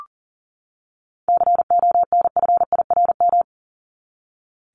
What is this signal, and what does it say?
Morse code, heard as a keyed tone, about 40 wpm, 700 Hz. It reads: CONFIRM
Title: Top 1751-2000 Words 40wpm